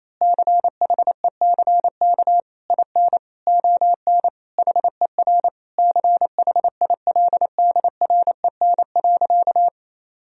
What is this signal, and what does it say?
Morse code, heard as a keyed tone, about 28 wpm, 700 Hz. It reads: CHECK IN ON HER CHILDREN.
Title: Sentences from Top 1000 Words - Review Part 3 - 28wpm